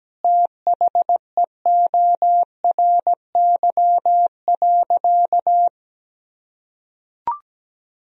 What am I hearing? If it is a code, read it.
Morse code, heard as a keyed tone, about 17 wpm, 700 Hz. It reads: THEORY.
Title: Sentences from Top 1000 Words - Review Part 3 - 17wpm